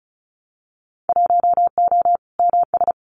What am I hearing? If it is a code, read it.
Morse code, heard as a keyed tone, about 34 wpm, 690 Hz. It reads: 1O MS
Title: Winter Field Day Exchanges 35wpm